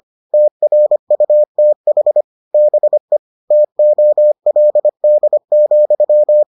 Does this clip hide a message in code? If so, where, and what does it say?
Yes: Morse code from 0.3 to 6.4 s, heard as a keyed tone, about 25 wpm, 600 Hz. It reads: TRUTH BE TOLD,